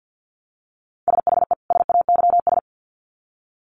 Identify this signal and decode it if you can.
Morse code, heard as a keyed tone, about 48 wpm, 700 Hz. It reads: SHE SAYS
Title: Sets of 2 Words 50wpm